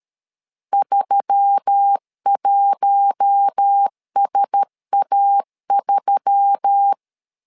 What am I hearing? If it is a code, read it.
Morse code, heard as a keyed tone, about 13 wpm, 800 Hz. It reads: 31SA3
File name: LEZIONI / 4 Q1S3 12wpm 2023